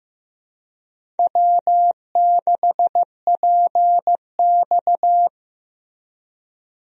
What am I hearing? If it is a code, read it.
Morse code, heard as a keyed tone, about 15 wpm, 700 Hz. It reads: W6PX
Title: USA Call Signs - 4 Characters - Repeated 3x - 15wpm